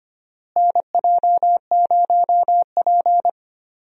Morse code, heard as a keyed tone, about 25 wpm, 700 Hz. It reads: NJ0P
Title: USA Call Signs - 4 Characters - Repeated 3x - 25wpm